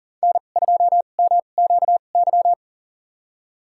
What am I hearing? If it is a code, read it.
Morse code, heard as a keyed tone, about 37 wpm, spaced as for 28 wpm, 700 Hz. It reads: N2MQY
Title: USA Call Signs - 5 Characters - 2x Character Spacing - 40wpm